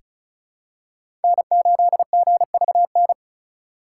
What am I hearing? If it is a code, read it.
Morse code, heard as a keyed tone, about 34 wpm, 700 Hz. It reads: N8GVN
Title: USA Call Signs - 5 Characters - 6x Character Spacing - 35wpm